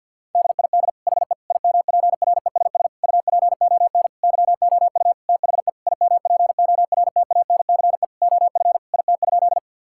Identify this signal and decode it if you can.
Morse code, heard as a keyed tone, about 50 wpm, 700 Hz. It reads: DID HE IMPRESS UPON YOU THE IMPORTANCE OF IT?
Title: Sentences from Top 1600 Words 50wpm